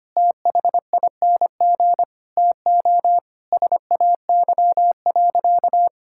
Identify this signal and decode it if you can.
Morse code, heard as a keyed tone, about 25 wpm, 700 Hz. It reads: THING TO SAY.
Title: Sentences from Top 1000 Words - Review Part 5 - 25wpm